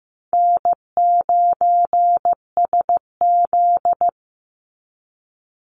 Morse code, heard as a keyed tone, about 15 wpm, 700 Hz. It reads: N9SZ
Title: Easy Call Signs - 15wpm